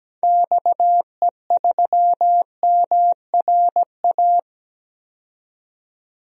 Morse code, heard as a keyed tone, about 17 wpm, 700 Hz. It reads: XE3MRA
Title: DXCC-100 Call Signs - 6 Characters - 17wpm